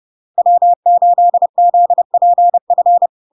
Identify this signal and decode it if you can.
Morse code, heard as a keyed tone, about 30 wpm, 700 Hz. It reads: W8ZPF
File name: LARK-FD-2022-Log-in-CW-30wpm-700-Hz